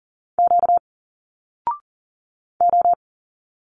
Morse code, heard as a keyed tone, about 41 wpm, 700 Hz. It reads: Q O
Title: Single Letters 40wpm